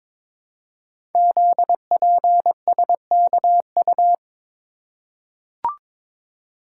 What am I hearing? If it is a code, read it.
Morse code, heard as a keyed tone, about 22 wpm, 700 Hz. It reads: ZPSKU
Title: Code Groups - 5 Letter:Numbers - 22wpm